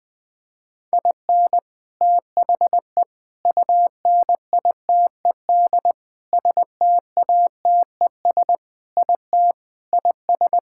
Morse code, heard as a keyed tone, about 20 wpm, 700 Hz. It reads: IN THE UNITED STATES IT IS
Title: Sentences with 3-Word Phrases - 20wpm